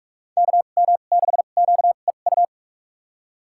Morse code, heard as a keyed tone, about 47 wpm, spaced as for 33 wpm, 700 Hz. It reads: KM6QEU
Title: USA Call Signs - 2x Character Spacing - 45wpm